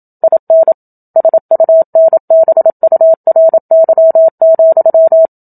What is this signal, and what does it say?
Morse code, heard as a keyed tone, about 27 wpm, 650 Hz. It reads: IN SUNBURY,
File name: Book Two / 01-Chapter 1 - 27 WPM